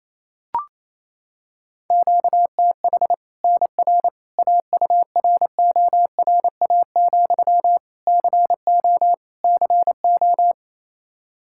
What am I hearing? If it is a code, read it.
Morse code, heard as a keyed tone, about 28 wpm, 700 Hz. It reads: QTH NR AURORA, CO CO
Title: Practice QSO - Exchange 1 - Beginning - 28wpm